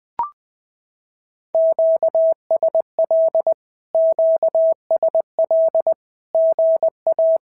Morse code, heard as a keyed tone, about 20 wpm, 650 Hz. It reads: QSL QSL GA
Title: SOTA QSO Practice 20wpm